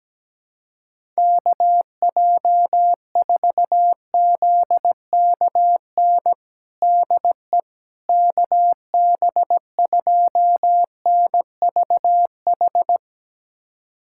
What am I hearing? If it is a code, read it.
Morse code, heard as a keyed tone, about 17 wpm, 700 Hz. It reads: KJ4ZKN DE KB2NVH
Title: USA Callsign DE Callsign - 6 Characters - 17wpm